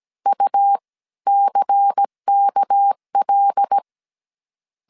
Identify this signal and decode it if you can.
Morse code, heard as a keyed tone, about 17 wpm, 800 Hz. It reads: U CKL
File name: 16wpm